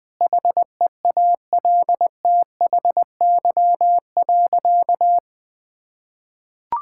Morse code, heard as a keyed tone, about 20 wpm, 700 Hz. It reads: HEALTHY.
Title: Sentences from Top 1000 Words - Review Part 5 - 20wpm